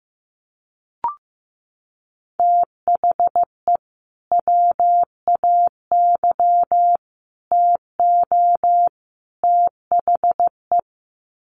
Morse code, heard as a keyed tone, about 15 wpm, 700 Hz. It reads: THE WAY TO THE
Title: Sets of 4 Words 15wpm